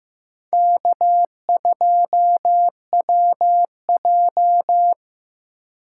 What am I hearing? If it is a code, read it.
Morse code, heard as a keyed tone, about 15 wpm, 700 Hz. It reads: K2WJ
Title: USA Call Signs - 4 Characters - 4x Character Spacing - 15wpm